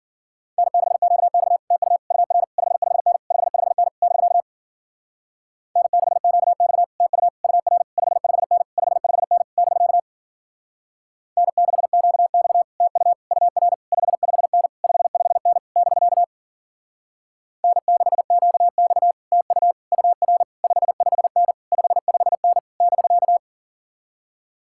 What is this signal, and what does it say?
Morse code, heard as a keyed tone, about 48 wpm, 700 Hz. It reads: N6QX TU UR 55N 55N <BK> N6QX TU UR 55N 55N <BK> N6QX TU UR 55N 55N <BK> N6QX TU UR 55N 55N <BK>
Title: Speed-Racing Common QSO Phrases - 40wpm